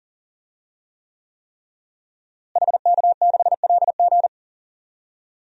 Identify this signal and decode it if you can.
Morse code, heard as a keyed tone, about 37 wpm, 700 Hz. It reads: HK6LG
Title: DXCC-100 Call Signs - 5 Characters - Repeated 3x - 40wpm